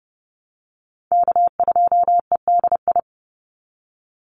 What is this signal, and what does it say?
Morse code, heard as a keyed tone, about 30 wpm, 700 Hz. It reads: K2EDI
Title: USA Call Signs 30wpm